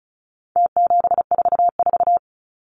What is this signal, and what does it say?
Morse code, heard as a keyed tone, about 34 wpm, 700 Hz. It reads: T744
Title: Numbers - Contest Style - 4-Digit 35wpm